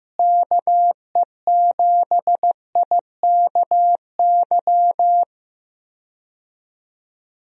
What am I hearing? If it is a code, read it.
Morse code, heard as a keyed tone, about 15 wpm, 700 Hz. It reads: KE7IKY
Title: USA Call Signs - 6 Characters - Repeated 2x - 15wpm